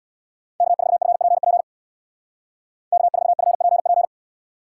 Morse code, heard as a keyed tone, about 61 wpm, 700 Hz. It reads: K4FPP K4FPP
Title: Speed Racing - USA Call Signs 50wpm